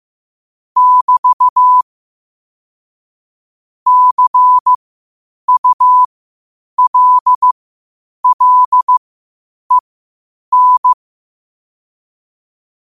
Morse code, heard as a keyed tone, about 15 wpm, spaced as for 8 wpm, 1000 Hz. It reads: <BT> CULLEN
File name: 8wpm